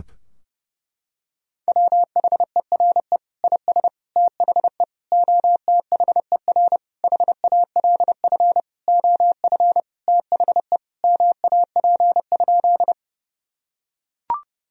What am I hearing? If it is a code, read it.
Morse code, heard as a keyed tone, about 32 wpm, 700 Hz. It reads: WHERE IS THE OTHER HALF OF THE MAP?
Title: Sentences from Top 1000 Words - Review Part 2 - 30wpm